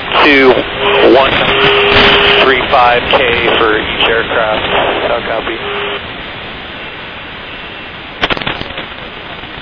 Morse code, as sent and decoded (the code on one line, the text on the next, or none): none